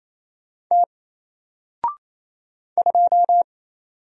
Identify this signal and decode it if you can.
Morse code, heard as a keyed tone, about 27 wpm, 700 Hz. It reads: T 2
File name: Single-Letter-Number-28wpm